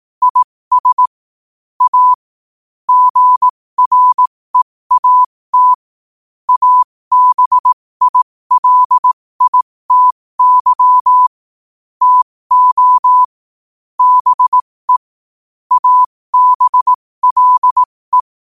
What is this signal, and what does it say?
Morse code, heard as a keyed tone, about 18 wpm, spaced as for 15 wpm, 1000 Hz. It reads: IS A GREAT ABILITY TO BE ABLE
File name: qotdfiles / 15wpm